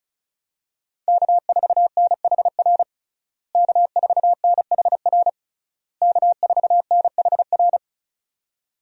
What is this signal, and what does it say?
Morse code, heard as a keyed tone, about 34 wpm, 700 Hz. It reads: K4NHR K4NHR K4NHR
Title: USA Call Signs - 5 Characters - Repeated 3x - 35wpm